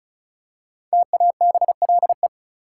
Morse code, heard as a keyed tone, about 34 wpm, 700 Hz. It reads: TABLE
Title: Five Letter Words - 35wpm